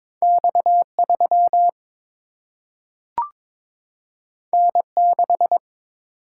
Morse code, heard as a keyed tone, about 22 wpm, 700 Hz. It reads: X 3 N 6
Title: Single-Letter Single-Number 22wpm